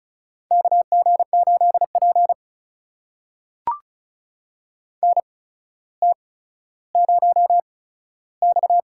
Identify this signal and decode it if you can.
Morse code, heard as a keyed tone, about 36 wpm, spaced as for 15 wpm, 700 Hz. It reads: KG8P NT0X
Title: USA Call Signs - 4 Characters - 8x Character Spacing - 35wpm